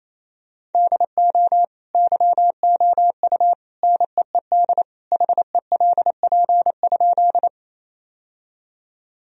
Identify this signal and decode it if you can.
Morse code, heard as a keyed tone, about 28 wpm, 700 Hz. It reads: DO YOU NEED HELP?
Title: Sentences from Top 1000 Words - Review Part 1 - 28wpm